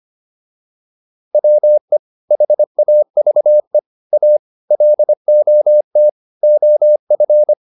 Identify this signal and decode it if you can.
Morse code, heard as a keyed tone, about 25 wpm, 600 Hz. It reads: WE HAVE A LOT OF